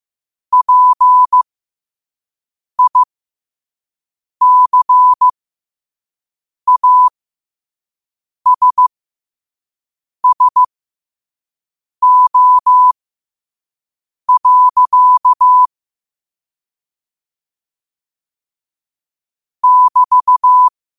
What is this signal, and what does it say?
Morse code, heard as a keyed tone, about 15 wpm, spaced as for 5 wpm, 1000 Hz. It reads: PICASSO. <BT>
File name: qotdfiles / 5wpm